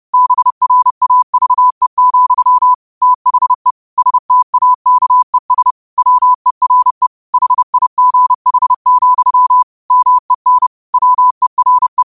Morse code, heard as a keyed tone, about 30 wpm, 1000 Hz. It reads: BRAVE, THE STAKES WERE HIGH, MEN WERE
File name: qotdfiles / 30wpm